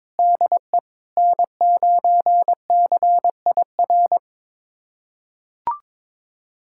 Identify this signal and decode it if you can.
Morse code, heard as a keyed tone, about 22 wpm, 700 Hz. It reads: DE N9CIR
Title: USA Callsign DE Callsign - 5 Characters - 22wpm